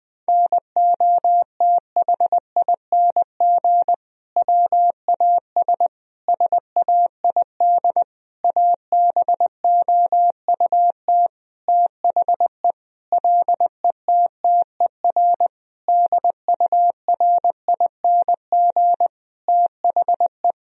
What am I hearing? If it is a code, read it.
Morse code, heard as a keyed tone, about 21 wpm, 700 Hz. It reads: NOTHING WAS SAID ABOUT THE LETTER DURING THE
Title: Sentences from Top 1000 Words - Review Part 2 - 20wpm